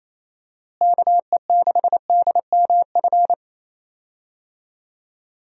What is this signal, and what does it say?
Morse code, heard as a keyed tone, about 28 wpm, 700 Hz. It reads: KE6DMF
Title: USA Call Signs - 6 Characters - Repeated 3x - 28wpm